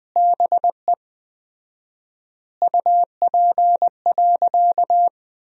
Morse code, heard as a keyed tone, about 20 wpm, 700 Hz. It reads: BE UP.
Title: 4x Word-Spacing - Sentences from Top 300 Words - 20wpm